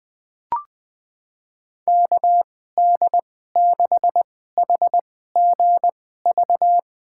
Morse code, heard as a keyed tone, about 21 wpm, spaced as for 15 wpm, 700 Hz. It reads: KD6HGV
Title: USA Call Signs - 6 Characters - 2x Character Spacing - 20wpm